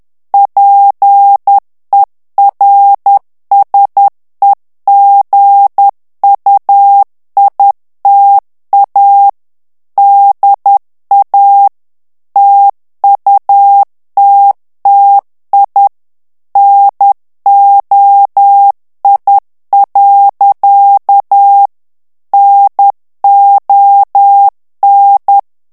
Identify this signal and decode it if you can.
Morse code, heard as a keyed tone, about 10 wpm, 800 Hz. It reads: PERSEGUITA DA TUTTI NOI. NON
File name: morse mp3 / ITALIEN texte clair 1 vitesse 10 mots minute